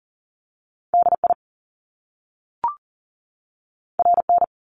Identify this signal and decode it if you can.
Morse code, heard as a keyed tone, about 39 wpm, 700 Hz. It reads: DI RN
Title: Top Two Letter Combinations in Words 40wpm